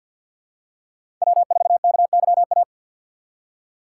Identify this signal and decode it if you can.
Morse code, heard as a keyed tone, about 52 wpm, 690 Hz. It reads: W4KYA
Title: USA Call Signs - 5 Characters - 3x Character Spacing - 50wpm